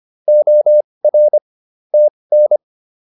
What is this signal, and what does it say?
Morse code, heard as a keyed tone, about 25 wpm, spaced as for 20 wpm, 600 Hz. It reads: OR TN